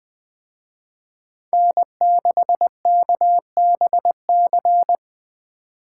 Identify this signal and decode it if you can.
Morse code, heard as a keyed tone, about 21 wpm, 700 Hz. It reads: N6KBC